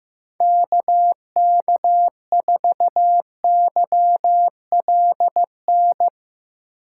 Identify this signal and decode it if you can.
Morse code, heard as a keyed tone, about 15 wpm, 700 Hz. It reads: KK4YLN
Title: USA Call Signs - 6 Characters - 3x Character Spacing - 15wpm